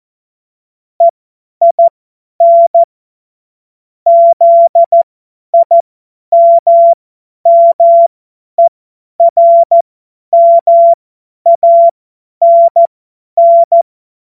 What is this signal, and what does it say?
Morse code, heard as a keyed tone, about 14 wpm, spaced as for 10 wpm, 680 Hz. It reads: EIN ZIMMERMANN
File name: Morsekurs / Witze / Witz 02. Dezember 2025 50 F70